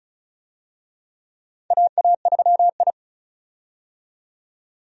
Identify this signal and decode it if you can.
Morse code, heard as a keyed tone, about 34 wpm, 700 Hz. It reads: AA3I
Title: USA Call Signs - 4 Characters - 35wpm